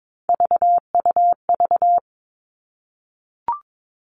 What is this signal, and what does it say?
Morse code, heard as a keyed tone, about 22 wpm, 700 Hz. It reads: VUV
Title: Three Challenging Letter-Number Combinations 22wpm